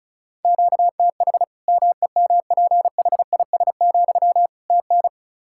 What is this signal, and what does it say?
Morse code, heard as a keyed tone, about 36 wpm, 700 Hz. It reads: QTH MEMPHIS, TN
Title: Practice QSO - Exchange 1 - Beginning - 35wpm